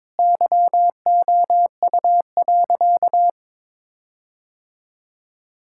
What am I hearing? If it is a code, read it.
Morse code, heard as a keyed tone, about 22 wpm, 700 Hz. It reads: YOU.
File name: practice / 22wpm / Sentences with 8-Word Phrases - 22wpm